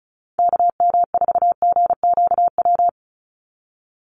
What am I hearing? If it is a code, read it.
Morse code, heard as a keyed tone, about 34 wpm, 700 Hz. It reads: KM4GQW
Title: USA Call Signs 35wpm